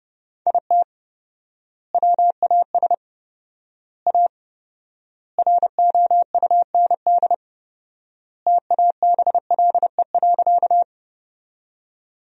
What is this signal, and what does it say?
Morse code, heard as a keyed tone, about 31 wpm, 700 Hz. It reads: IT WAS A ROUND TABLE.
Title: 4x Word-Spacing - Sentences from Top 400 Words - 30wpm